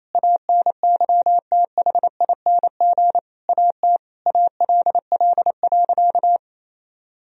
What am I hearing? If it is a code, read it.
Morse code, heard as a keyed tone, about 28 wpm, 700 Hz. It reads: ANYTHING AT ALL.
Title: Sentences from Top 1000 Words - Review Part 1 - 28wpm